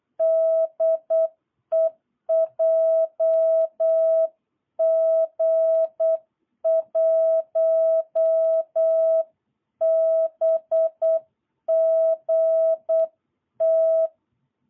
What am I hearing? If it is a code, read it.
Morse code, heard as a keyed tone, about 8 wpm, 650 Hz. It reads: DE JG1BGT
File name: jh8cbh / 2022cw